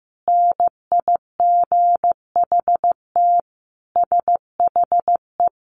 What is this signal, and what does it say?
Morse code, heard as a keyed tone, about 15 wpm, 700 Hz. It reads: NIGHT SHE
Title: Sentences from Top 300 Words 15wpm